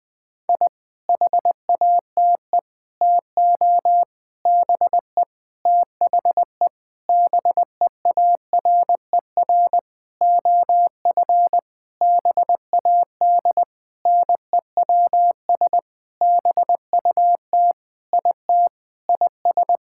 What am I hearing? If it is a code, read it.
Morse code, heard as a keyed tone, about 20 wpm, 700 Hz. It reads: I HATE TO BE THE BEARER OF BAD NEWS BUT IT IS
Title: Sentences with 9-Word Phrases - Spoken Beginning - 20wpm